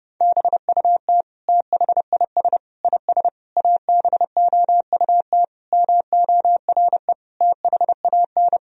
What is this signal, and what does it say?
Morse code, heard as a keyed tone, about 30 wpm, 700 Hz. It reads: BUT THIS IS ABOUT MORE THAN
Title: Sentences from Top 1000 Words - Review Part 3 - 30wpm